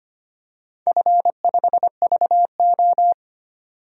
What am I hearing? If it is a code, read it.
Morse code, heard as a keyed tone, about 25 wpm, 700 Hz. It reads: F5VO
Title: DXCC-100 Call Signs - 4 Characters - 25wpm